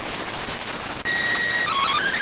SSTV (slow-scan television), PD 290.